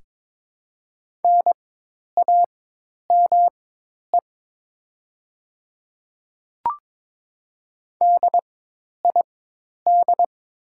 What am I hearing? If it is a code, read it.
Morse code, heard as a keyed tone, about 22 wpm, spaced as for 10 wpm, 700 Hz. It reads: NAME DID
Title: 4x Character Spacing - Top 200 Words - 22wpm